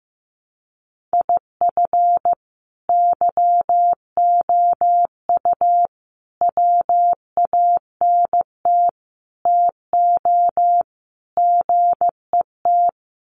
Morse code, heard as a keyed tone, about 15 wpm, 700 Hz. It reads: IF YOU WANT TO GET
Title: Sets of 5 Words Encore 15wpm